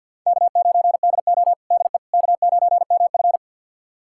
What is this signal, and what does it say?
Morse code, heard as a keyed tone, about 52 wpm, 700 Hz. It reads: K9DO DE K9MF
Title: USA Callsign DE Callsign - 4 Characters - 50wpm